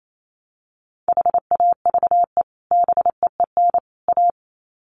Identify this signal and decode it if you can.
Morse code, heard as a keyed tone, about 28 wpm, 700 Hz. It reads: HAVE BEEN A
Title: Sets of 3 Words 28wpm